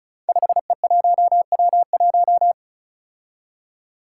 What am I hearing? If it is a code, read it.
Morse code, heard as a keyed tone, about 36 wpm, 700 Hz. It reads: 5E1W1
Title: Five Challenging Letter-Number Combinations 35wpm